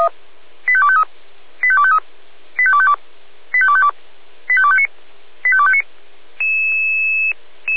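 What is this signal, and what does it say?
Five-tone selective call (ZVEI), heard as quick runs of five tones, tones of about 70 ms. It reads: 85253 85253 85252 85252 85259 85259
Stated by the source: Received on SDR, ZVEI code 85280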